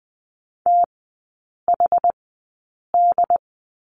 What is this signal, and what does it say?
Morse code, heard as a keyed tone, about 20 wpm, spaced as for 8 wpm, 690 Hz. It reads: THD
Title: 081 Lesson 14e - Number 2 - Three Characters 20wpm